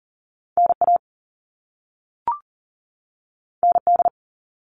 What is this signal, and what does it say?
Morse code, heard as a keyed tone, about 38 wpm, 700 Hz. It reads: NA ND
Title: All Cut Numbers - 2-Digit 40wpm